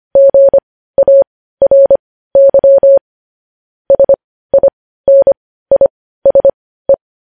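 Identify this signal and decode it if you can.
Morse code, heard as a keyed tone, about 25 wpm, spaced as for 15 wpm, 550 Hz. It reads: GARY SINISE
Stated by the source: News headlines sent as Morse practice at 15.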